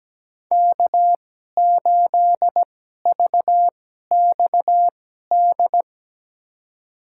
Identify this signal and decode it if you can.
Morse code, heard as a keyed tone, about 17 wpm, spaced as for 12 wpm, 700 Hz. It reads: K8VXD
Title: USA Call Signs - 5 Characters - 2x Character Spacing - 17wpm